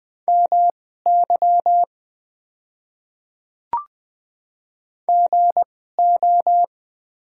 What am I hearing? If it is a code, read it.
Morse code, heard as a keyed tone, about 21 wpm, spaced as for 15 wpm, 700 Hz. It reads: MY GO
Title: 2x Character Spacing - Top 100 Words - 20wpm